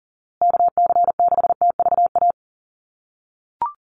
Morse code, heard as a keyed tone, about 40 wpm, 700 Hz. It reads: KC6TVA
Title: USA Call Signs Repeated 3x 40wpm